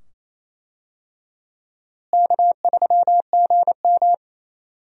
Morse code, heard as a keyed tone, about 28 wpm, 700 Hz. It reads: K3GM
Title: USA Call Signs - 4 Characters - Repeated 2x - 28wpm